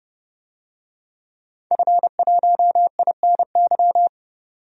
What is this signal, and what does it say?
Morse code, heard as a keyed tone, about 30 wpm, 690 Hz. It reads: F1INY
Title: DXCC-100 Call Signs - 5 Characters - 30wpm